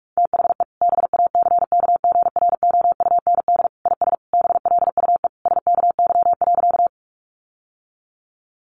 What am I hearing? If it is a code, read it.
Morse code, heard as a keyed tone, about 46 wpm, 700 Hz. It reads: THE BACKGROUND IS BLUE SKY.
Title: Sentences from Top 1300 Words 45wpm